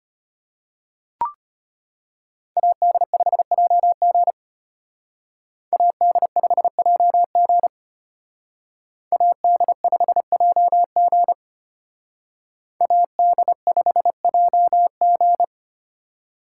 Morse code, heard as a keyed tone, about 32 wpm, 700 Hz. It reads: AD5JG AD5JG AD5JG AD5JG